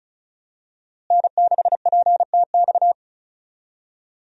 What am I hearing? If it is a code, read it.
Morse code, heard as a keyed tone, about 36 wpm, 700 Hz. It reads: N6PTX
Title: USA Call Signs - 5 Characters - Repeated 3x - 35wpm